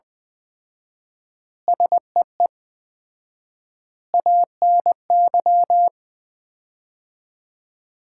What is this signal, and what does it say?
Morse code, heard as a keyed tone, about 21 wpm, 700 Hz. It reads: SEE ANY
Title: 4x Word-Spacing - Sets of 5 Words - 20wpm